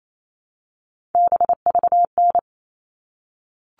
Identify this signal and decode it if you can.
Morse code, heard as a keyed tone, about 28 wpm, 700 Hz. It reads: BVN
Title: All Cut Numbers - 3-Digit 28wpm